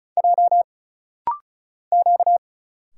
Morse code, heard as a keyed tone, about 36 wpm, 700 Hz. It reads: J Q
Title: Single Letters - Rapid-Fire 35wpm